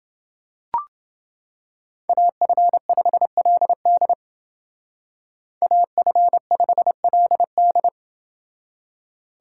Morse code, heard as a keyed tone, about 29 wpm, 700 Hz. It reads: AF5LD AF5LD